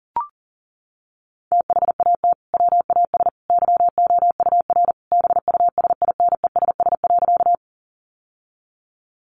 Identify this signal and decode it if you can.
Morse code, heard as a keyed tone, about 40 wpm, 700 Hz. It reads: THAT WAS YOUR BUSINESS.
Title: Sentences from Top 600 Words 40wpm